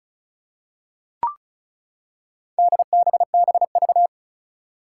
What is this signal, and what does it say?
Morse code, heard as a keyed tone, about 36 wpm, 700 Hz. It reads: DBBV
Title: Four Challenging Letter-Number Combinations 35wpm